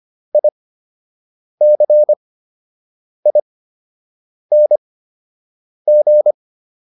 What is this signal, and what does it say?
Morse code, heard as a keyed tone, about 25 wpm, spaced as for 7 wpm, 600 Hz. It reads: ICING